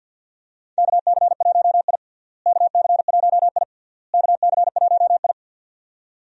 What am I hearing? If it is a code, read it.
Morse code, heard as a keyed tone, about 52 wpm, 700 Hz. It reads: KC1I KC1I KC1I
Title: USA Call Signs - 4 Characters - Repeated 3x - 50wpm